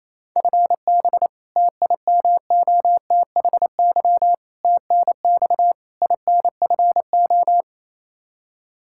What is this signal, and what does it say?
Morse code, heard as a keyed tone, about 28 wpm, 700 Hz. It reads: FB TIMOTHY TNX INFO
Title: Practice QSO - Exchange 3 - More Info - 28wpm